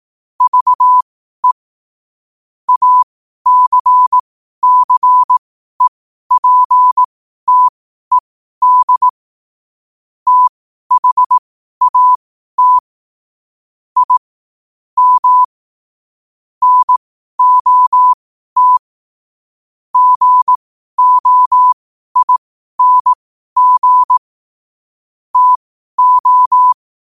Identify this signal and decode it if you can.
Morse code, heard as a keyed tone, about 18 wpm, spaced as for 12 wpm, 1000 Hz. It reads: VE ACCEPTED THAT I M NOT GOING TO
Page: Quotes for Sun, 27 Apr 2025 in Morse Code at 12 words per minute.